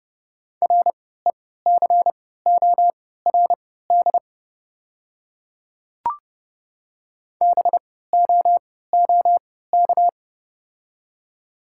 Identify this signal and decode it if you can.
Morse code, heard as a keyed tone, about 30 wpm, spaced as for 17 wpm, 690 Hz. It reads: RECORD BOOK
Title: 3x Character Spacing - Top 500 Words - 30wpm